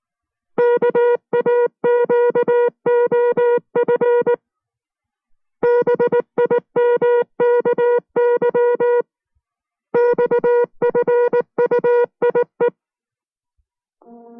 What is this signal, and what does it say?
Morse code, heard as a keyed tone, about 19 wpm, 480 Hz. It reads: KAQOF BIMKY XFUIE